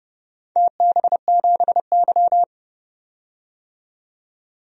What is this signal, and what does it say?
Morse code, heard as a keyed tone, about 29 wpm, 700 Hz. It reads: TB7Y
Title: DXCC-100 Call Signs - 4 Characters - 30wpm